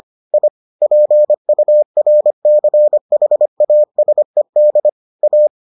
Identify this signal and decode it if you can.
Morse code, heard as a keyed tone, about 25 wpm, 600 Hz. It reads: I PURCHASED A